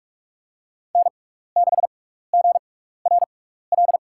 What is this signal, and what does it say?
Morse code, heard as a keyed tone, about 44 wpm, spaced as for 15 wpm, 700 Hz. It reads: N6GRL
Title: USA Call Signs - 6x Character Spacing - 45wpm